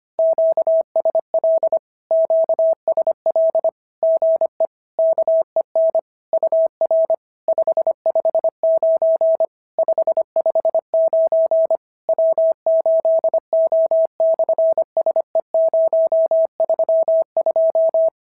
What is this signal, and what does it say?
Morse code, heard as a keyed tone, about 25 wpm, 650 Hz. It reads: QSL QSL GE KEN UR 559 559 W8O/SE032
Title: SOTA QSO Practice 25wpm